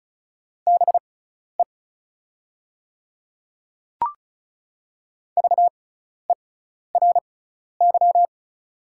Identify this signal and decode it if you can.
Morse code, heard as a keyed tone, about 36 wpm, spaced as for 12 wpm, 690 Hz. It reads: BE VERY
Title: 6x Character Spacing - Top 200 Words - 35wpm